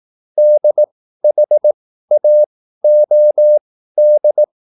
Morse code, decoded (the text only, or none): DHAOD